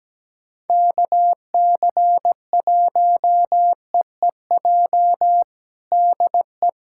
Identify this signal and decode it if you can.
Morse code, heard as a keyed tone, about 17 wpm, 700 Hz. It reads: KC1EEJ DE
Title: USA Callsign DE Callsign - 6 Characters - 17wpm